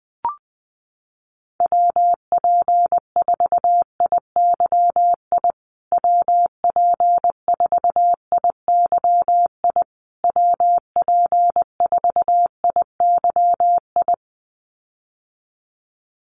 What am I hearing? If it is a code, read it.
Morse code, heard as a keyed tone, about 20 wpm, 700 Hz. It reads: WP4IYI WP4IYI WP4IYI
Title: USA Call Signs Repeated 3x 20wpm